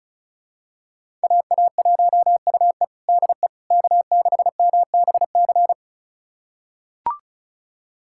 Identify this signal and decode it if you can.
Morse code, heard as a keyed tone, about 36 wpm, 700 Hz. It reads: AA1UE DE K6MBC
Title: USA Callsign DE Callsign - 5 Characters - 35wpm